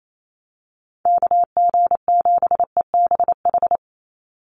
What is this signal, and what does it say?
Morse code, heard as a keyed tone, about 28 wpm, 700 Hz. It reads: KG7EBH
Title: USA Call Signs 28wpm